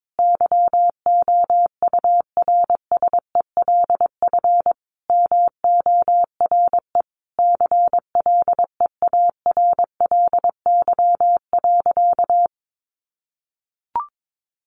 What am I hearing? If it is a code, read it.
Morse code, heard as a keyed tone, about 22 wpm, 700 Hz. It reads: YOURSELF MORE CLEARLY.
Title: Sentences from Top 1300 Words 22wpm